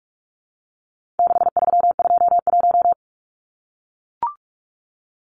Morse code, heard as a keyed tone, about 44 wpm, 700 Hz. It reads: 6321
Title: Numbers - 4-Digit 45wpm